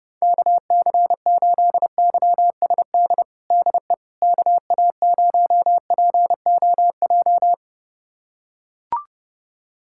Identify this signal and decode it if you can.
Morse code, heard as a keyed tone, about 32 wpm, 700 Hz. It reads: KC8YSD DE KA0POJ
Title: USA Callsign DE Callsign - 6 Characters - 30wpm